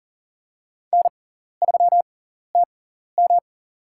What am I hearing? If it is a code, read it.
Morse code, heard as a keyed tone, about 40 wpm, spaced as for 14 wpm, 700 Hz. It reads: N3TM
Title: USA Call Signs - 4 Characters - 6x Character Spacing - 40wpm